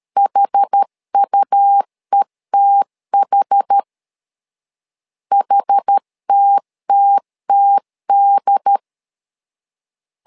Morse code, heard as a keyed tone, about 13 wpm, 800 Hz. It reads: HUETH HTTTD
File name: LEZIONI / 2 UHDN / 2 UHD 12wpm 1540